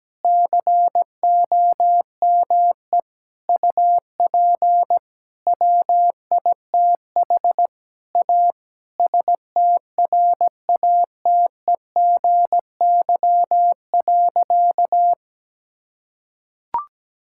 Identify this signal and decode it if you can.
Morse code, heard as a keyed tone, about 17 wpm, 700 Hz. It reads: COME UP WITH A STRATEGY.
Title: Sentences from Top 1000 Words - Review Part 5 - 17wpm